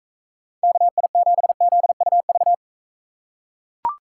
Morse code, heard as a keyed tone, about 42 wpm, 700 Hz. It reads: KI7ZUV